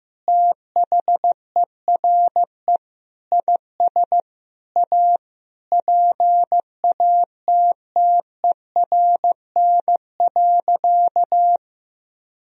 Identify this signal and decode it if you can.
Morse code, heard as a keyed tone, about 15 wpm, 700 Hz. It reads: THERE IS A PATTERN.
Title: Sentences from Top 1000 Words - Review Part 2 - 15wpm